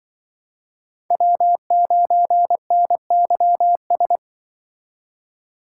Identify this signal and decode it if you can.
Morse code, heard as a keyed tone, about 24 wpm, 700 Hz. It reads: W9NYS
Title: Speed Racing - USA 1x3 Call Signs 20wpm